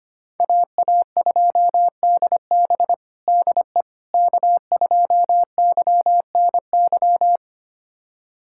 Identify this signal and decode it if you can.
Morse code, heard as a keyed tone, about 25 wpm, 700 Hz. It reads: AA2DB DE K2YNY
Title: USA Callsign DE Callsign - 5 Characters - 25wpm